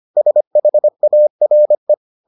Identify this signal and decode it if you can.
Morse code, heard as a keyed tone, about 25 wpm, 600 Hz. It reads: SHARE